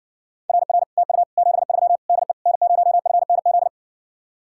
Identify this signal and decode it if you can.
Morse code, heard as a keyed tone, about 60 wpm, 700 Hz. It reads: RR TU 73 DE N0FNZ